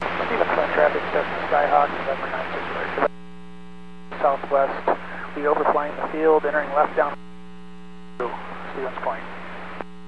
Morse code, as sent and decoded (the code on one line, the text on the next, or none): none